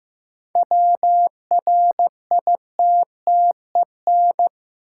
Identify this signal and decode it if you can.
Morse code, heard as a keyed tone, about 15 wpm, 700 Hz. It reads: WRITTEN
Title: Sentences from Top 1500 Words - New Word Drill Alt - 15wpm